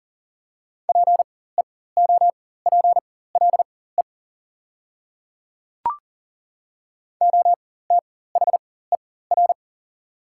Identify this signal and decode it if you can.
Morse code, heard as a keyed tone, about 40 wpm, spaced as for 19 wpm, 700 Hz. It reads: PEOPLE OTHER
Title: 4x Character Spacing - Top 200 Words - 40wpm